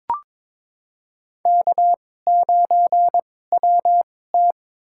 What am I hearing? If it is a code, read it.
Morse code, heard as a keyed tone, about 22 wpm, spaced as for 16 wpm, 700 Hz. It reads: K9WT
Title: USA Call Signs - 4 Characters - 2x Character Spacing - 22wpm